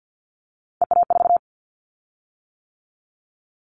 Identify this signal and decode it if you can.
Morse code, heard as a keyed tone, about 50 wpm, 700 Hz. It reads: EA4
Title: Callsign Prefixes for Top 25 DXCC - 50wpm